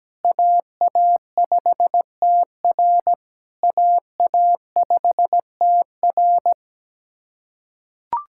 Morse code, heard as a keyed tone, about 17 wpm, 700 Hz. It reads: AA5TR AA5TR
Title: SOTA QSO Practice 17wpm